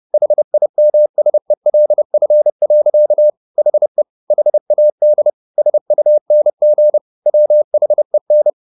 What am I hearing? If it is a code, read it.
Morse code, heard as a keyed tone, about 30 wpm, 600 Hz. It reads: HIMSELF. HE HAD SUNG WHEN
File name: the old man and the sea / the old man and the sea 1 30wpm